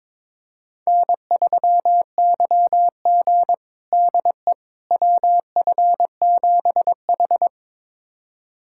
Morse code, heard as a keyed tone, about 22 wpm, 700 Hz. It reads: N3YG DE WF7H
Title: USA Callsign DE Callsign - 4 Characters - 22wpm